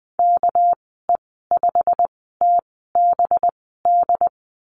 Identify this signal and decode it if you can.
Morse code, heard as a keyed tone, about 20 wpm, spaced as for 14 wpm, 700 Hz. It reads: KE5TBD
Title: 231 Lesson 37g - Letter x - Call Signs 20wpm